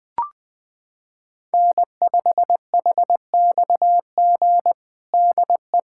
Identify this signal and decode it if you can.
Morse code, heard as a keyed tone, about 20 wpm, 700 Hz. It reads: N5HXG DE
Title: USA Callsign DE Callsign - 5 Characters - 20wpm